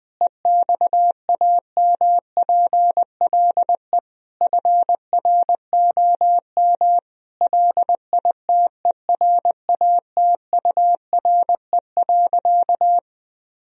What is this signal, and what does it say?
Morse code, heard as a keyed tone, about 21 wpm, 700 Hz. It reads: EXAMPLE FROM LITERATURE.
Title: Sentences from Top 1000 Words - Review Part 3 - 20wpm